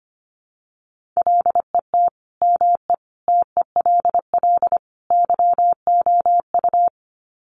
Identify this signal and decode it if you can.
Morse code, heard as a keyed tone, about 25 wpm, 700 Hz. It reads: LET ME TELL YOU
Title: Sets of 4 Words 25wpm